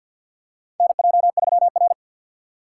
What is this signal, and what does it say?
Morse code, heard as a keyed tone, about 50 wpm, 700 Hz. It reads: NJ3R
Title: USA Call Signs - 4 Characters - 2x Character Spacing - 50wpm